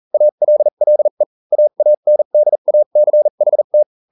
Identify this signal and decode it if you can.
Morse code, heard as a keyed tone, about 40 wpm, spaced as for 35 wpm, 600 Hz. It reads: ALLE AANDACHT